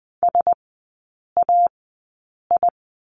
Morse code, heard as a keyed tone, about 20 wpm, spaced as for 8 wpm, 700 Hz. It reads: SAI
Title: 035 Lesson 7e - Letter s - Three Characters 20wpm